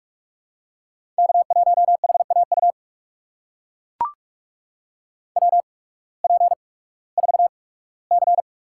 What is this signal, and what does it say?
Morse code, heard as a keyed tone, about 42 wpm, spaced as for 21 wpm, 700 Hz. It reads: K1HAU WP4C
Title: USA Call Signs - 8x Character Spacing - 45wpm